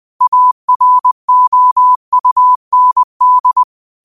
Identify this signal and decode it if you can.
Morse code, heard as a keyed tone, about 20 wpm, 1000 Hz. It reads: AROUND
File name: qotdfiles / 20wpm